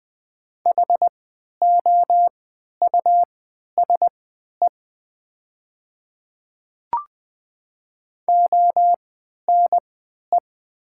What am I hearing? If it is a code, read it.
Morse code, heard as a keyed tone, about 20 wpm, spaced as for 11 wpm, 690 Hz. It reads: HOUSE ONE
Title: 3x Character Spacing - Top 200 Words - 20wpm